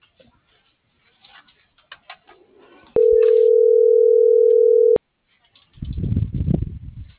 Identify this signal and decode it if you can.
Ringback tone (North America), 440 + 480 Hz.